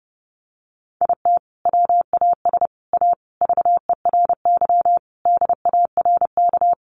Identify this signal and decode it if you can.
Morse code, heard as a keyed tone, about 28 wpm, 700 Hz. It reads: IT WAS A VERY DARK
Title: Sentences from Top 500 Words 30wpm